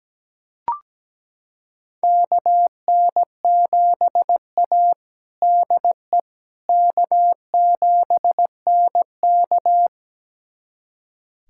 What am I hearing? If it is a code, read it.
Morse code, heard as a keyed tone, about 17 wpm, 700 Hz. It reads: KN7A DE K7NK
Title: USA Callsign DE Callsign - 4 Characters - 17wpm